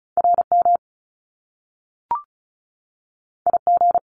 Morse code, heard as a keyed tone, about 36 wpm, 700 Hz. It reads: RM IG
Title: Top Two Letter Combinations in Words 35wpm